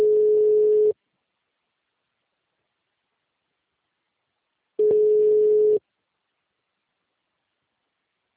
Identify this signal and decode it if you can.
Ringback tone (Europe, 425 Hz).